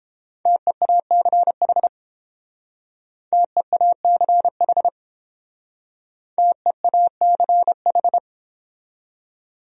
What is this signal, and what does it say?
Morse code, heard as a keyed tone, about 31 wpm, 700 Hz. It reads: TEACH TEACH TEACH